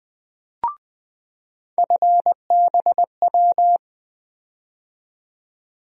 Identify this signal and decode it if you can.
Morse code, heard as a keyed tone, about 20 wpm, 700 Hz. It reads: FBW
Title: USA Call Sign Postfixes - 3 Letters 20wpm